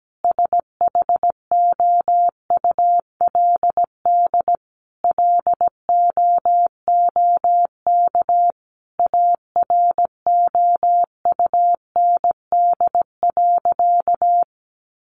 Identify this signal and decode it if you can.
Morse code, heard as a keyed tone, about 17 wpm, 700 Hz. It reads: SHOULD LOOK AROUND.
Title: Sentences from Top 800 Words 17wpm